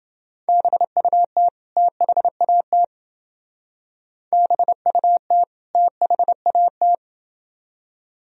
Morse code, heard as a keyed tone, about 28 wpm, 700 Hz. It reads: BUT THAT BUT THAT